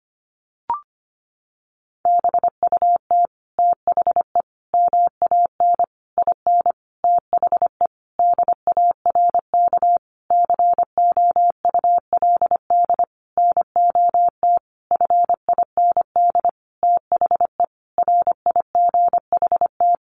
Morse code, heard as a keyed tone, about 25 wpm, 700 Hz. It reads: BUT THE MAN IN THE DARK COULD NOT FIND THE RIGHT
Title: Sentences from Top 600 Words 25wpm